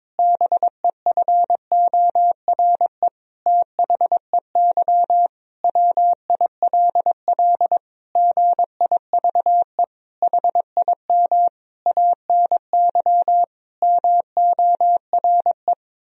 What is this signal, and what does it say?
Morse code, heard as a keyed tone, about 22 wpm, 700 Hz. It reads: BEFORE THEY WILL GIVE HIM ANY MORE
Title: Sentences from Top 1000 Words - Review Part 4 - 22wpm